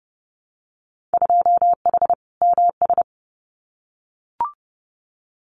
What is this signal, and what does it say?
Morse code, heard as a keyed tone, about 30 wpm, 700 Hz. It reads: 2H MS
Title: Winter Field Day Exchanges 30wpm